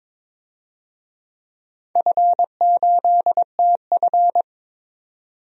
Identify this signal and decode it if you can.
Morse code, heard as a keyed tone, about 22 wpm, 700 Hz. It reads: F8TF
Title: DXCC-100 Call Signs - 4 Characters - 22wpm